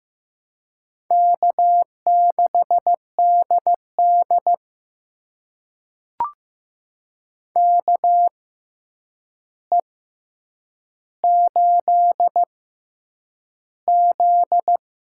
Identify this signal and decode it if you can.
Morse code, heard as a keyed tone, about 15 wpm, spaced as for 8 wpm, 700 Hz. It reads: K6DD KE8Z
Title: USA Call Signs - 6x Character Spacing - 15wpm